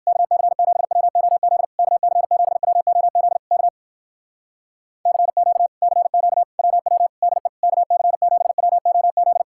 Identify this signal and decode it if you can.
Morse code, heard as a keyed tone, about 59 wpm, 700 Hz. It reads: KC7WOZ KC7WOZ K CQ CQ WW DE KC7WOZ